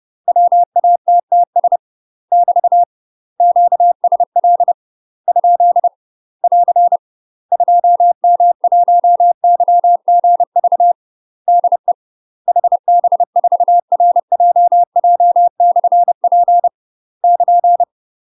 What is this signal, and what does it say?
Morse code, heard as a keyed tone, about 30 wpm, 700 Hz. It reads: WATTS <BT> QSL ? <AR> 2M1YGV DE HB4RJJ/P <KN>
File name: UK MASTER QSOs TRAINING / 30WPM / UK MASTER QSOs 0049